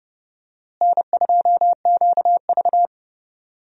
Morse code, heard as a keyed tone, about 30 wpm, 700 Hz. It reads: N2QV
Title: USA Call Signs - 4 Characters - Repeated 2x - 30wpm